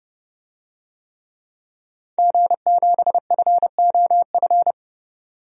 Morse code, heard as a keyed tone, about 31 wpm, 700 Hz. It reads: G7FOF
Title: DXCC-100 Call Signs - 5 Characters - 30wpm